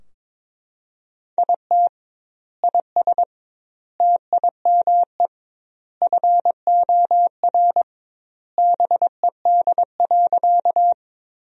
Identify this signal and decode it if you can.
Morse code, heard as a keyed tone, about 22 wpm, 700 Hz. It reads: IT IS TIME FOR BED.
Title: 2x Word-Spacing - Sentences from Top 500 Words - 22wpm